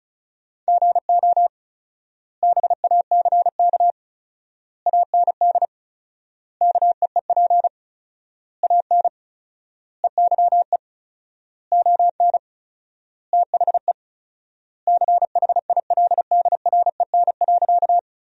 Morse code, heard as a keyed tone, about 36 wpm, 700 Hz. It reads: GO BACK AND KEEP AN EYE ON THE CHILDREN.
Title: 4x Word-Spacing - Sentences from Top 300 Words - 35wpm